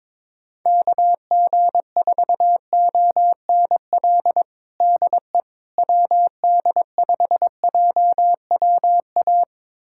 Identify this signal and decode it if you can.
Morse code, heard as a keyed tone, about 22 wpm, 700 Hz. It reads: KG4ONL DE WD5JWA
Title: USA Callsign DE Callsign - 6 Characters - 22wpm